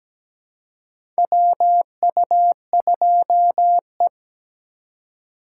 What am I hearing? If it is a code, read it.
Morse code, heard as a keyed tone, about 17 wpm, 700 Hz. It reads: WU2E
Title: USA Call Signs - 4 Characters - 2x Character Spacing - 17wpm